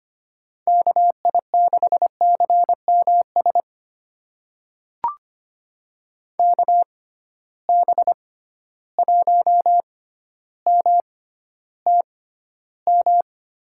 Morse code, heard as a keyed tone, about 25 wpm, spaced as for 13 wpm, 700 Hz. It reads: KI6CMS KB1MTM
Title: USA Call Signs - 6 Characters - 6x Character Spacing - 25wpm